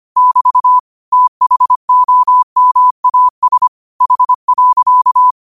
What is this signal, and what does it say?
Morse code, heard as a keyed tone, about 25 wpm, 1000 Hz. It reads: <BT> THOMAS H.